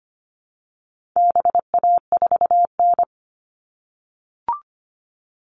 Morse code, heard as a keyed tone, about 25 wpm, 700 Hz. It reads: BA4N
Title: All Cut Numbers - 4-Digit 25wpm